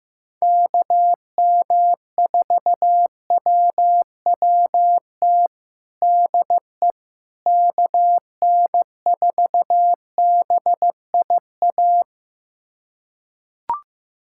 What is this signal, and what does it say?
Morse code, heard as a keyed tone, about 15 wpm, 700 Hz. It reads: KM4WWT DE KN4BIA
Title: USA Callsign DE Callsign 15wpm